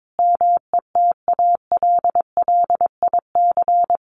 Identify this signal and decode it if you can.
Morse code, heard as a keyed tone, about 22 wpm, 700 Hz. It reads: METALLIC
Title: A Sound Of Thunder 22wpm